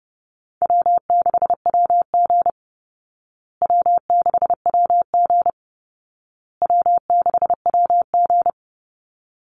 Morse code, heard as a keyed tone, about 30 wpm, 700 Hz. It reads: W6WG W6WG W6WG
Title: Easy Call Signs Repeated 3x - 30wpm